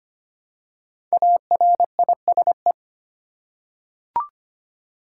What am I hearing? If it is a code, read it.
Morse code, heard as a keyed tone, about 25 wpm, 700 Hz. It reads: ARISE
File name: 5 Letter Words - First 3 Letters Given - 25wpm